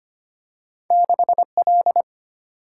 Morse code, heard as a keyed tone, about 25 wpm, 700 Hz. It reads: 6L
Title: Single-Letter Single-Number 25wpm